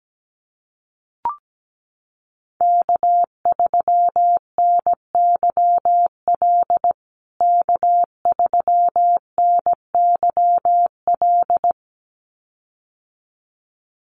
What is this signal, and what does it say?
Morse code, heard as a keyed tone, about 17 wpm, 700 Hz. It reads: K3NYL K3NYL
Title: USA Call Signs Repeated 2x 17wpm